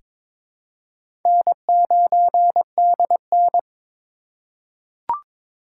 Morse code, heard as a keyed tone, about 23 wpm, 700 Hz. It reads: N9DN
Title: USA Call Signs - 4 Characters - 4x Character Spacing - 22wpm